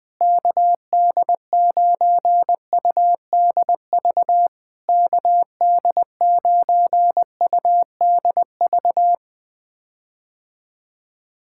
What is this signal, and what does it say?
Morse code, heard as a keyed tone, about 20 wpm, 700 Hz. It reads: KD9UDV KD9UDV
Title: POTA QSO Practice - 20wpm